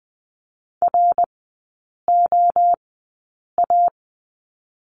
Morse code, heard as a keyed tone, about 19 wpm, spaced as for 8 wpm, 700 Hz. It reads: ROA
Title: 100 Lesson 17e - Letter u - Three Characters 20wpm